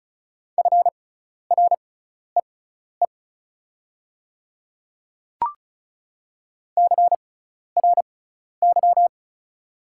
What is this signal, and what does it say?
Morse code, heard as a keyed tone, about 34 wpm, spaced as for 12 wpm, 700 Hz. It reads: FREE CRY
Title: 6x Character Spacing - Top 500 Words - 35wpm